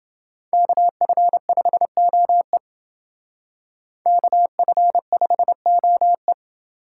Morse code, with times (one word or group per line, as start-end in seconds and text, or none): 0.5-2.6 KF5OE
4.1-6.3 KF5OE